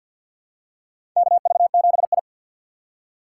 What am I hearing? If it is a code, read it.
Morse code, heard as a keyed tone, about 52 wpm, 700 Hz. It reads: KV7I
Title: USA Call Signs - 4 Characters - 2x Character Spacing - 50wpm